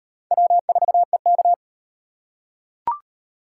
Morse code, heard as a keyed tone, about 32 wpm, 700 Hz. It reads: W4EK